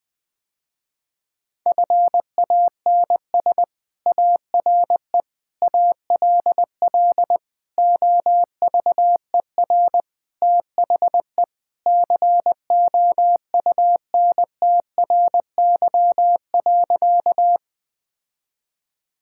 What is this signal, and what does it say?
Morse code, heard as a keyed tone, about 20 wpm, 700 Hz. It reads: FANS ARE ALL OVER THE COUNTRY.
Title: Sentences with 4-Word Phrases - Spoken Beginning - 20wpm